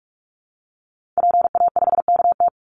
Morse code, heard as a keyed tone, about 44 wpm, 700 Hz. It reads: PA5KT
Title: Call Signs Repeated 3x - 45wpm